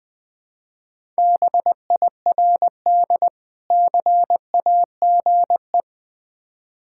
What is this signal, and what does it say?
Morse code, heard as a keyed tone, about 20 wpm, 700 Hz. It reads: BIRD CAGE
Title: Difficult Words with Hints - 20wpm